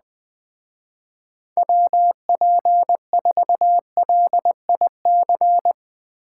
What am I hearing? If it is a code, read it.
Morse code, heard as a keyed tone, about 20 wpm, 700 Hz. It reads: WP4LIC